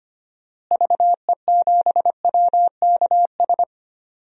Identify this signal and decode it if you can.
Morse code, heard as a keyed tone, about 25 wpm, 700 Hz. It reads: VE7WKS
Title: DXCC-100 Call Signs - 6 Characters - Repeated 3x - 25wpm